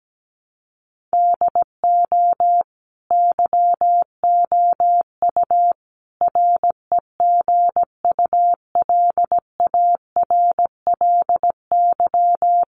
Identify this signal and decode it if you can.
Morse code, heard as a keyed tone, about 17 wpm, 700 Hz. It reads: DO YOU REGULARLY
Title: Sentences from Top 1400 Words 17wpm